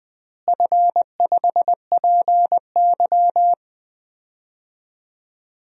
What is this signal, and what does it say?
Morse code, heard as a keyed tone, about 21 wpm, 700 Hz. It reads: F5PY
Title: DXCC-100 Call Signs - 4 Characters - Repeated 2x - 20wpm